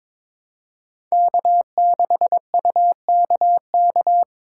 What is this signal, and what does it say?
Morse code, heard as a keyed tone, about 22 wpm, 700 Hz. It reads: K6UKK
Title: USA Call Signs - 5 Characters - 4x Character Spacing - 22wpm